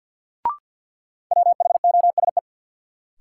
Morse code, heard as a keyed tone, about 53 wpm, 700 Hz. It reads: WHOSE
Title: Sentences from Top 1200 Words - New Word Drill Alt - 50wpm